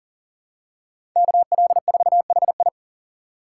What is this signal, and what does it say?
Morse code, heard as a keyed tone, about 40 wpm, 700 Hz. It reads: KL4HI
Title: USA Call Signs - 4x Character Spacing - 40wpm